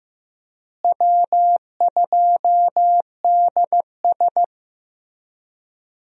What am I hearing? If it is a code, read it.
Morse code, heard as a keyed tone, about 15 wpm, 700 Hz. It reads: W2DS
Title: USA Call Signs - 4 Characters - Repeated 3x - 15wpm